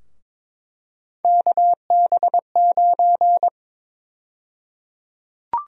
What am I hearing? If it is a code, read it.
Morse code, heard as a keyed tone, about 23 wpm, 700 Hz. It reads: KB9
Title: USA Call Sign Prefixes - Repeated 2x - 22wpm